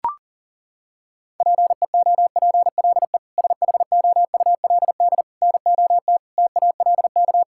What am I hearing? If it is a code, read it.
Morse code, heard as a keyed tone, about 40 wpm, 700 Hz. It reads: PEOPLE SHOULD NOT TALK
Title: Sentences from Top 1000 Words - Review Part 6 - 40wpm